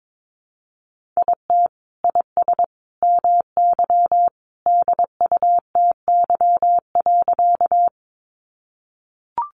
Morse code, heard as a keyed tone, about 22 wpm, 700 Hz. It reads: IT IS MY DUTY.
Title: Sentences from Top 1500 Words 22wpm